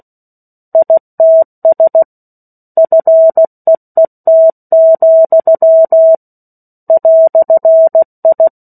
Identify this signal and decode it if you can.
Morse code, heard as a keyed tone, about 16 wpm, 650 Hz. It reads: ITS FEET, "I
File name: 03-Chapter-03 16 WPM